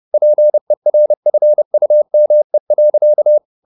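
Morse code, heard as a keyed tone, about 30 wpm, 600 Hz. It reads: PERFUME.
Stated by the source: Morse code at 30 wpm, the novel Never by Ken Follett in English, from CW med Gnister.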